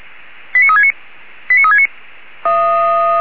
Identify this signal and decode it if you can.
Five-tone selective call (ZVEI), heard as quick runs of five tones, tones of about 70 ms. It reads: 89369 89369